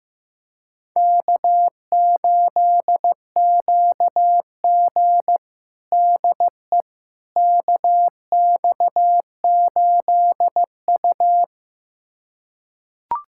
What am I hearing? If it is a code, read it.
Morse code, heard as a keyed tone, about 16 wpm, 700 Hz. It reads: K8QG DE KX8U
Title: USA Callsign DE Callsign - 4 Characters - 15wpm